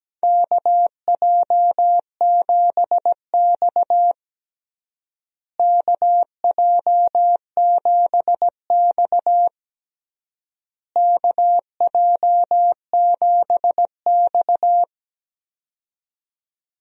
Morse code, heard as a keyed tone, about 17 wpm, 700 Hz. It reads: KJ7X KJ7X KJ7X
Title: USA Call Signs - 4 Characters - Repeated 3x - 17wpm